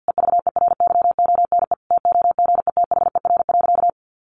Morse code, heard as a keyed tone, about 50 wpm, 700 Hz. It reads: EVERYONE TOGETHER.
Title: Sentences from Top 1500 Words 50wpm